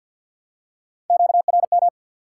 Morse code, heard as a keyed tone, about 53 wpm, 700 Hz. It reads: QRM
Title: Top 180 QSO Elements 50wpm